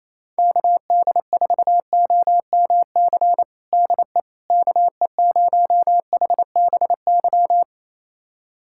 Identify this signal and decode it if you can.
Morse code, heard as a keyed tone, about 28 wpm, 700 Hz. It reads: KD4OMC DE KE0HBY
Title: USA Callsign DE Callsign - 6 Characters - 28wpm